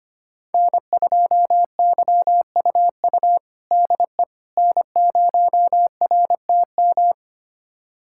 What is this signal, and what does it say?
Morse code, heard as a keyed tone, about 25 wpm, 700 Hz. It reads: N2YUU DE N0RTM
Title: USA Callsign DE Callsign - 5 Characters - 25wpm